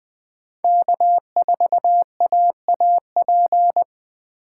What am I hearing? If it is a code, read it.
Morse code, heard as a keyed tone, about 20 wpm, 700 Hz. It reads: K4AAP
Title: USA Call Signs - 5 Characters - 6x Character Spacing - 20wpm